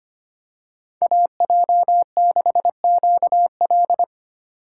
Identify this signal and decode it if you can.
Morse code, heard as a keyed tone, about 25 wpm, 700 Hz. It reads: AJ6QL
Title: K1USN SST QSO Practice - 25wpm